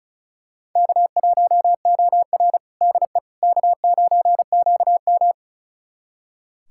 Morse code, heard as a keyed tone, about 36 wpm, 700 Hz. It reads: K1OR DE K9QM
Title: USA Callsign DE Callsign - 4 Characters - 35wpm